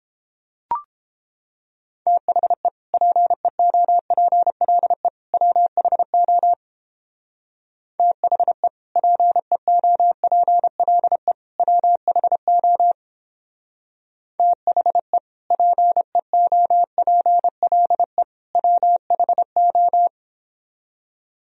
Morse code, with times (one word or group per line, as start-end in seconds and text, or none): 2.1-2.7 THE
2.9-5.1 PEOPLE
5.3-6.5 WHO
8.0-8.7 THE
9.0-11.3 PEOPLE
11.6-12.9 WHO
14.4-15.2 THE
15.5-18.2 PEOPLE
18.5-20.1 WHO